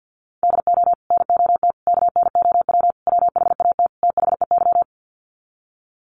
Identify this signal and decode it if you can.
Morse code, heard as a keyed tone, about 50 wpm, 700 Hz. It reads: DO NOT KNOW WHAT THEY
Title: Sets of 5 Words 50wpm